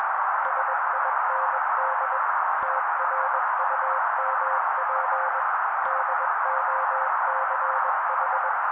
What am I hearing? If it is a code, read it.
Morse code, heard as a keyed tone, about 21 wpm, 530 Hz. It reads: SIND TRUMP DOCH
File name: daily / 20wpm noise